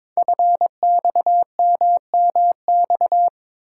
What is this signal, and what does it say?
Morse code, heard as a keyed tone, about 22 wpm, 700 Hz. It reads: FXMMX
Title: Code Groups - 5 Letters - 22wpm